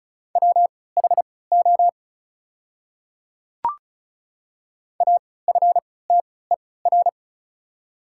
Morse code, heard as a keyed tone, about 34 wpm, spaced as for 20 wpm, 690 Hz. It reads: WHO AFTER
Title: 3x Character Spacing - Top 200 Words - 35wpm